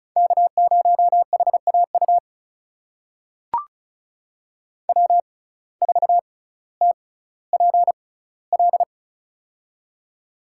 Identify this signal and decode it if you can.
Morse code, heard as a keyed tone, about 36 wpm, spaced as for 18 wpm, 700 Hz. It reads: K0HAU W4TPL
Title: USA Call Signs - 5 Characters - 6x Character Spacing - 35wpm